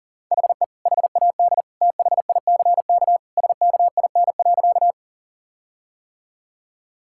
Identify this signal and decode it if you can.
Morse code, heard as a keyed tone, about 40 wpm, 700 Hz. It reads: HE HAD THICK SKIN.
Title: Sentences from Top 1600 Words 40wpm